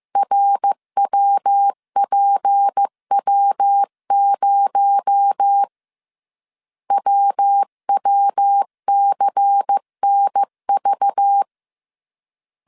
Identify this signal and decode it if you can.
Morse code, heard as a keyed tone, about 15 wpm, 800 Hz. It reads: RWPW0 WWCNV
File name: CW / DET / 1 6 1500C S C 14wpm 2951